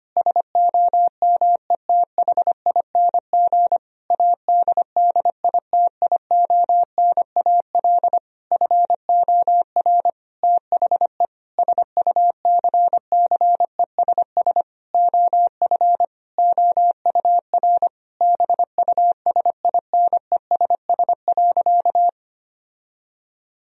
Morse code, heard as a keyed tone, about 25 wpm, 700 Hz. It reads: SOMETHING ADDITIONAL FOR THE SUCCESS OF OUR BUSINESS.
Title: Sentences from Top 1000 Words - Review Part 4 - 25wpm